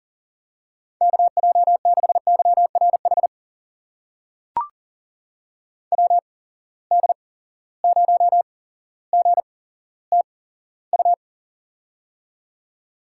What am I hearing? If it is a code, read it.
Morse code, heard as a keyed tone, about 40 wpm, spaced as for 17 wpm, 700 Hz. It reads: KJ6YRH WD0GTU
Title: USA Call Signs - 6 Characters - 8x Character Spacing - 40wpm